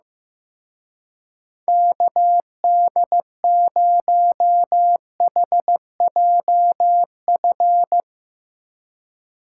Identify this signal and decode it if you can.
Morse code, heard as a keyed tone, about 16 wpm, 700 Hz. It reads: KD0HJF
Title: Speed Racing - USA 2x3 Call Signs 15wpm